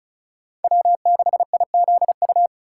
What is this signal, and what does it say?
Morse code, heard as a keyed tone, about 34 wpm, 690 Hz. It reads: W6IZU
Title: USA Call Signs - 4x Character Spacing - 35wpm